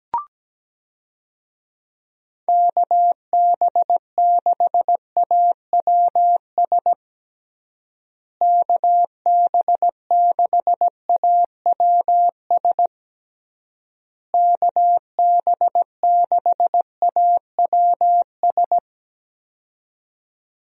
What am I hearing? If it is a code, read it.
Morse code, heard as a keyed tone, about 17 wpm, 700 Hz. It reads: KB6AWS KB6AWS KB6AWS
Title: USA Call Signs - 6 Characters - Repeated 3x - 17wpm